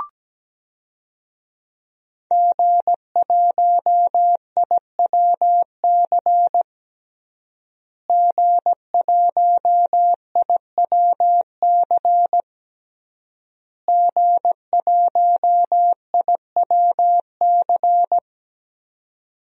Morse code, heard as a keyed tone, about 17 wpm, 700 Hz. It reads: G1IWC G1IWC G1IWC
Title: DXCC-100 Call Signs - 5 Characters - Repeated 3x - 17wpm